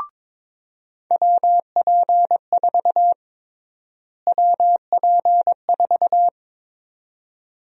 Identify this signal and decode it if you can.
Morse code, heard as a keyed tone, about 22 wpm, 700 Hz. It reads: WP4 WP4
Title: USA Call Sign Prefixes - Repeated 2x - 22wpm